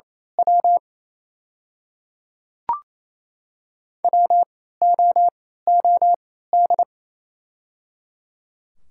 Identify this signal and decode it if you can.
Morse code, heard as a keyed tone, about 28 wpm, spaced as for 16 wpm, 700 Hz. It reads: W WOOD
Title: 3x Character Spacing - Top 500 Words - 28wpm